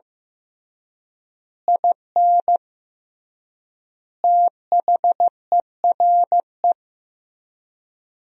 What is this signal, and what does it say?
Morse code, heard as a keyed tone, about 16 wpm, 700 Hz. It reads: IN THERE
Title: 3x Word-Spacing - Sentences from Top 500 Words - 15wpm